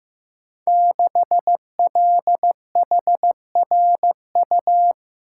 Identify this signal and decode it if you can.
Morse code, heard as a keyed tone, about 15 wpm, 700 Hz. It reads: 6LHRU
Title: Code Groups - 5 Letter:Numbers - 15wpm